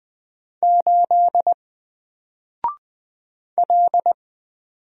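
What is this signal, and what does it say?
Morse code, heard as a keyed tone, about 20 wpm, 700 Hz. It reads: 8 L
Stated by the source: Single letter and number Morse drill at 20 wpm, from N4LCD.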